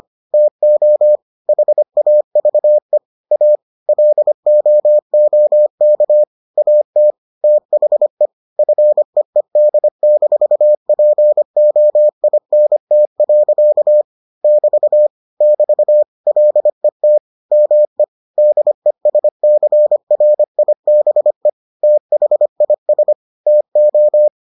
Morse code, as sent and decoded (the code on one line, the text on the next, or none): - --- / .... .- ...- . / .- / .-.. --- --- -.- / .- - / - .... / ..-. . -.. -....- .--. --- .. -. - .-.-.- / -...- / -...- / .-.. - / -- . / -.. ... -.-. .-. .. -... / - .... .. ... / - ---
TO HAVE A LOOK AT THE FEED-POINT. <BT> <BT> LET ME DESCRIBE THIS TO